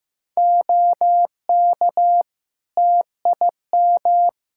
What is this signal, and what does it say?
Morse code, heard as a keyed tone, about 16 wpm, 700 Hz. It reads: OK TIM
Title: Practice QSO - Exchange 3 - More Info - 15wpm